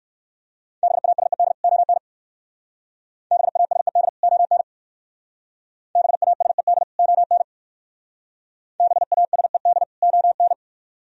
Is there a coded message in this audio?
Yes: Morse code from 0.8 to 10.5 s, heard as a keyed tone, about 59 wpm, 700 Hz. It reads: BASED ON BASED ON BASED ON BASED ON